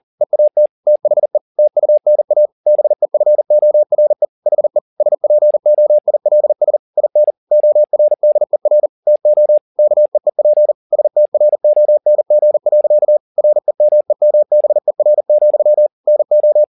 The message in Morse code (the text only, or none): EAT THE TUNA BEFORE HE SPOILS IN ORDER TO KEEP STRONG. REMEMBER, NO